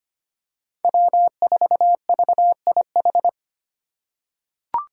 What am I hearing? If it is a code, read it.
Morse code, heard as a keyed tone, about 25 wpm, 700 Hz. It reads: W4VIH
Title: Code Groups - 5 Letter:Numbers - 25wpm